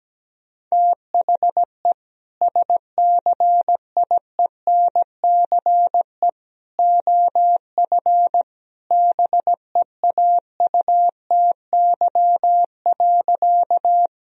Morse code, heard as a keyed tone, about 17 wpm, 700 Hz. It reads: THE SCIENCE OF BEAUTY.
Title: Sentences from Top 1000 Words - Review Part 2 - 17wpm